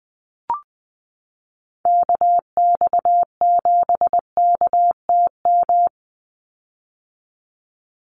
Morse code, heard as a keyed tone, about 20 wpm, 700 Hz. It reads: KX7KTM
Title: USA Call Signs 20wpm